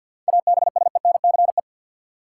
Morse code, heard as a keyed tone, about 50 wpm, 700 Hz. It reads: ABSENCE
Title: Sets of 4 Words - New Word Drill - 50wpm